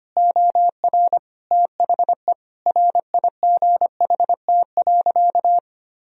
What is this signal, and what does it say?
Morse code, heard as a keyed tone, about 25 wpm, 700 Hz. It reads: OR THE RIGHT.
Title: Sentences from Top 1000 Words - Review Part 4 - 25wpm